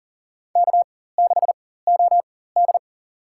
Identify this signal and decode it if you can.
Morse code, heard as a keyed tone, about 40 wpm, spaced as for 19 wpm, 700 Hz. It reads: K6OD
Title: USA Call Signs - 4 Characters - 4x Character Spacing - 40wpm